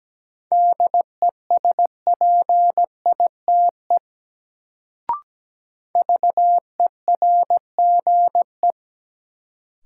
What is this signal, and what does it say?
Morse code, heard as a keyed tone, about 17 wpm, 700 Hz. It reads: DESPITE VERGE
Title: Sets of 4 Words - New Word Drill - 17wpm